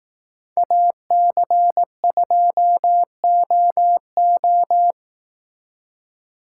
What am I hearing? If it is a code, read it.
Morse code, heard as a keyed tone, about 18 wpm, 700 Hz. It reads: AC2OO
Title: Speed Racing - USA 2x2 Call Signs 15wpm